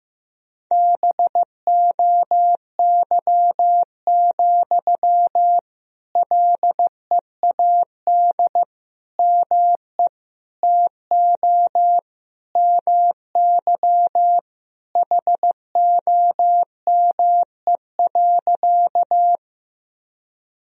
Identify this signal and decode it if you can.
Morse code, heard as a keyed tone, about 15 wpm, 700 Hz. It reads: BOY, LEAD ME TO MY HOME.
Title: Sentences from Top 1000 Words - Review Part 2 - 15wpm